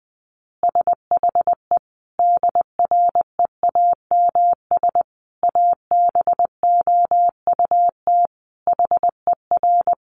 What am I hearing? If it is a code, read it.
Morse code, heard as a keyed tone, about 20 wpm, 700 Hz. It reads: SHE DREAMS ABOUT HER
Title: Easy Phrases 20wpm